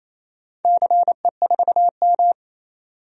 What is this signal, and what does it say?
Morse code, heard as a keyed tone, about 27 wpm, 700 Hz. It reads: CE4M
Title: DXCC-100 Call Signs - 4 Characters - 28wpm